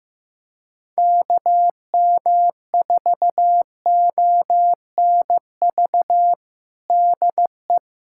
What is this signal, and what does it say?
Morse code, heard as a keyed tone, about 15 wpm, 700 Hz. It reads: KM4ONV DE
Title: USA Callsign DE Callsign - 6 Characters - 15wpm